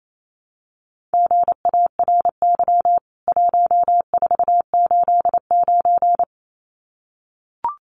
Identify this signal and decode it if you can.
Morse code, heard as a keyed tone, about 28 wpm, 700 Hz. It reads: GARY 1489
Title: CWT Exchange 28wpm